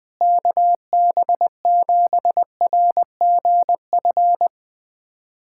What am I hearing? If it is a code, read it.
Morse code, heard as a keyed tone, about 20 wpm, 700 Hz. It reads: KB7RGF
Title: USA Callsign DE Callsign 20wpm